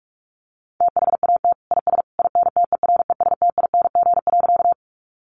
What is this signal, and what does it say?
Morse code, heard as a keyed tone, about 44 wpm, 700 Hz. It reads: THAT IS INTERESTING.
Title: Sentences from Top 1000 Words 45wpm